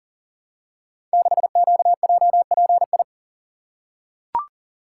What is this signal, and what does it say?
Morse code, heard as a keyed tone, about 39 wpm, 700 Hz. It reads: 6QJPI
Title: Code Groups - 5 Letter:Numbers - 40wpm